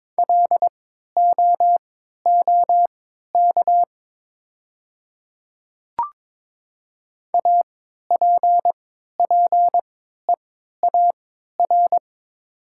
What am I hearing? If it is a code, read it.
Morse code, heard as a keyed tone, about 22 wpm, spaced as for 13 wpm, 700 Hz. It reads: LOOK APPEAR
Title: 3x Character Spacing - Top 500 Words - 22wpm